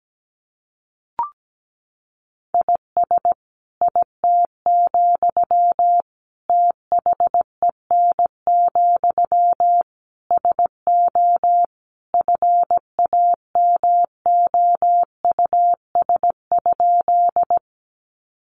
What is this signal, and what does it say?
Morse code, heard as a keyed tone, about 17 wpm, 700 Hz. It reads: IS IT, THEN, SO FAMOUS?
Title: Sentences from Top 1500 Words 17wpm